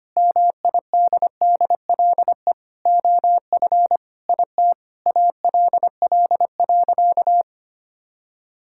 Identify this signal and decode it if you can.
Morse code, heard as a keyed tone, about 25 wpm, 700 Hz. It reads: MIDDLE OF IT ALL.
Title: Sentences from Top 1000 Words - Review Part 6 - 25wpm